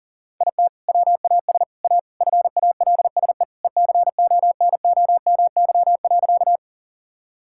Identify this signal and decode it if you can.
Morse code, heard as a keyed tone, about 40 wpm, 700 Hz. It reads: IT WAS A FALSE ECONOMY.
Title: Sentences from Top 1600 Words 40wpm